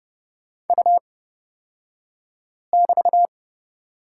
Morse code, heard as a keyed tone, about 31 wpm, spaced as for 5 wpm, 700 Hz. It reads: U<BT>
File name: Lesson 36d - BT - Two Characters 30wpm